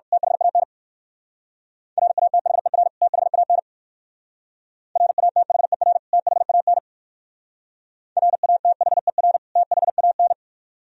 Morse code, heard as a keyed tone, about 54 wpm, 690 Hz. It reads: THAN RATHER THAN RATHER THAN RATHER THAN
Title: Speed-Racing - Sets of 2 Words 45wpm